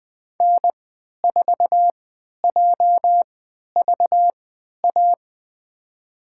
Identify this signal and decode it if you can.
Morse code, heard as a keyed tone, about 20 wpm, spaced as for 11 wpm, 700 Hz. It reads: N4JVA
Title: USA Call Signs - 5 Characters - 3x Character Spacing - 20wpm